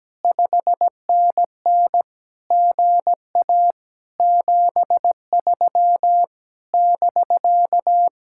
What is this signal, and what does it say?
Morse code, heard as a keyed tone, about 17 wpm, 700 Hz. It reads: 5NN GA 73 <BK>
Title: Speed-Racing Common QSO Phrases - 17wpm